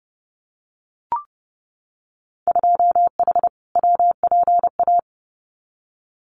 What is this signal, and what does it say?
Morse code, heard as a keyed tone, about 30 wpm, 700 Hz. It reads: 2H WPA
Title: Winter Field Day Exchanges 30wpm